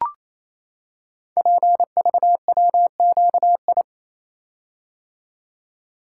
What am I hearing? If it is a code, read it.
Morse code, heard as a keyed tone, about 27 wpm, 700 Hz. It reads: PVWQI
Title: Code Groups - 5 Letters - 28wpm